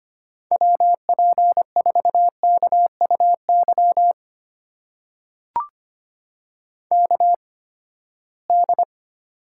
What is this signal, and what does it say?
Morse code, heard as a keyed tone, about 25 wpm, spaced as for 17 wpm, 700 Hz. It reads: WP4KUY KD
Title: USA Call Signs - 8x Character Spacing - 25wpm